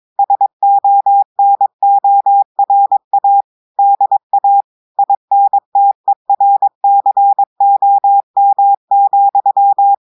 Morse code, heard as a keyed tone, about 22 wpm, 800 Hz. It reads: SONORA DA INTERCOM,
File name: audios / portugais / portugais texte en clair 1 vitesse 22 mots minutes